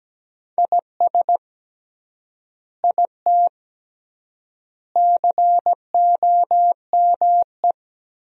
Morse code, heard as a keyed tone, about 17 wpm, 700 Hz. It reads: IS IT COME
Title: 3x Word-Spacing - Sentences from Top 500 Words - 17wpm